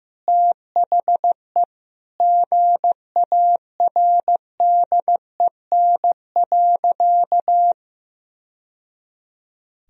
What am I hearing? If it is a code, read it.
Morse code, heard as a keyed tone, about 16 wpm, 700 Hz. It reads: THE GARDEN.
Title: Sentences from Top 1000 Words - Review Part 4 - 15wpm